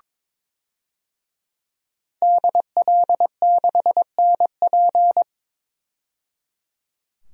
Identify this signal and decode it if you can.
Morse code, heard as a keyed tone, about 22 wpm, 700 Hz. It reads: DL6NP
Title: DXCC-100 Call Signs - 5 Characters - 22wpm